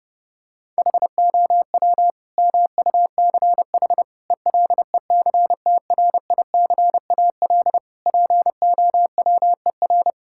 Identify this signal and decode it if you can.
Morse code, heard as a keyed tone, about 30 wpm, 700 Hz. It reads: HOW MUCH ELECTRICAL POWER
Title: Sentences from Top 1000 Words - Review Part 6 - 30wpm